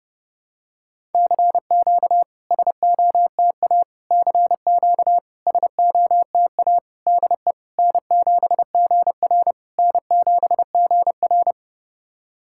Morse code, heard as a keyed tone, about 32 wpm, 700 Hz. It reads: CQ SOTA CQ SOTA DE N7GR N7GR
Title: SOTA QSO Practice 30wpm